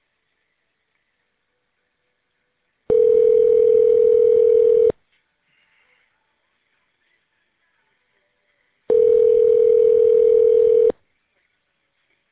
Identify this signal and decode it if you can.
Ringback tone (North America), 440 + 480 Hz.